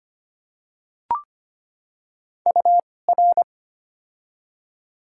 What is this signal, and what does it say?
Morse code, heard as a keyed tone, about 25 wpm, spaced as for 18 wpm, 700 Hz. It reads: UR
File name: Top 90 QSO Elements - 2x Character Spacing - 25wpm